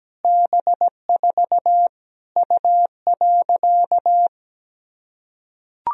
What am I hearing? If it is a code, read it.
Morse code, heard as a keyed tone, about 17 wpm, 700 Hz. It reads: B4 U.
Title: Sentences from Top 200 Words and Abbreviations 17wpm